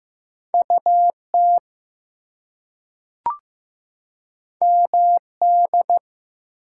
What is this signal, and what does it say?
Morse code, heard as a keyed tone, about 15 wpm, 700 Hz. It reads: UT MD
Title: US States and Canadian Provinces 15wpm